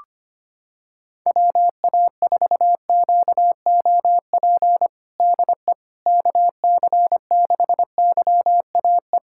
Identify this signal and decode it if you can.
Morse code, heard as a keyed tone, about 25 wpm, 700 Hz. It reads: WA4QOP DE KC6YAE
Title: USA Callsign DE Callsign 25wpm